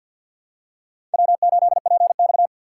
Morse code, heard as a keyed tone, about 46 wpm, 700 Hz. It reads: W8PX
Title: USA Call Signs - 4 Characters - 6x Character Spacing - 50wpm